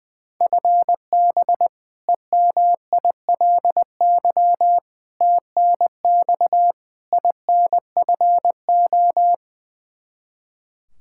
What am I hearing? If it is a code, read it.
Morse code, heard as a keyed tone, about 20 wpm, 700 Hz. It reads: FB EMILY TNX INFO
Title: Practice QSO - Exchange 3 - More Info - 20wpm